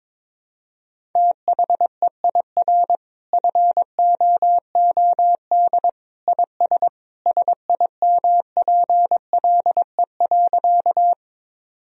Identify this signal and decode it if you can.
Morse code, heard as a keyed tone, about 21 wpm, 700 Hz. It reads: THEIR FOOD IS SIMPLE.
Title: Sentences from Top 1000 Words - Review Part 1 - 22wpm